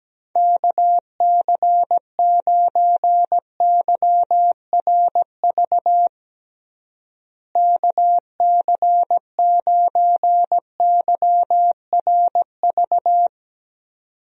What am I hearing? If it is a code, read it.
Morse code, heard as a keyed tone, about 17 wpm, 700 Hz. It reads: KC9YRV KC9YRV
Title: USA Call Signs - 6 Characters - Repeated 2x - 17wpm